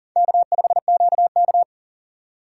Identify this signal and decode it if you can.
Morse code, heard as a keyed tone, about 40 wpm, 700 Hz. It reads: K5QK
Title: USA Call Signs - 4 Characters - 4x Character Spacing - 40wpm